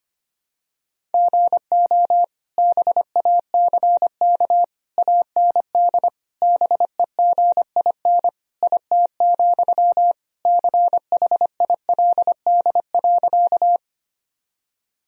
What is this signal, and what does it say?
Morse code, heard as a keyed tone, about 25 wpm, 700 Hz. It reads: GO BACK AND BEGIN IT, CHILD.
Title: Sentences from Top 1000 Words - Review Part 1 - 25wpm